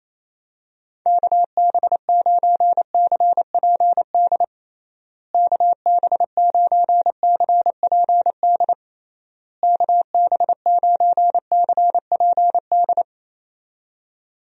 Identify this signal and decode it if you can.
Morse code, heard as a keyed tone, about 28 wpm, 700 Hz. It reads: KB9CPD KB9CPD KB9CPD
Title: USA Call Signs - 6 Characters - Repeated 3x - 28wpm